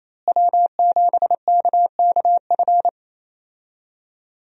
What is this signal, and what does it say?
Morse code, heard as a keyed tone, about 27 wpm, 700 Hz. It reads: W7KKF
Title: USA Call Signs - 5 Characters - Repeated 2x - 28wpm